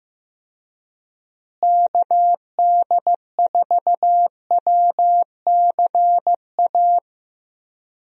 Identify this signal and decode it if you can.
Morse code, heard as a keyed tone, about 15 wpm, 700 Hz. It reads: KD4WCA
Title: USA Call Signs - 6 Characters - Repeated 2x - 15wpm